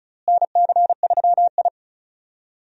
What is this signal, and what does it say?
Morse code, heard as a keyed tone, about 36 wpm, 700 Hz. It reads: NC3I
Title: USA Call Signs - 4 Characters - Repeated 2x - 35wpm